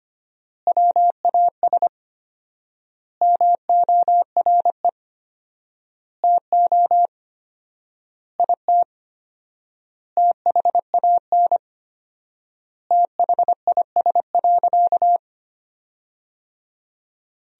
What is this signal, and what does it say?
Morse code, heard as a keyed tone, about 25 wpm, 700 Hz. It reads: WAS MORE TO IT THAN THIS.
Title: 4x Word-Spacing - Sentences from Top 100 Words - 25wpm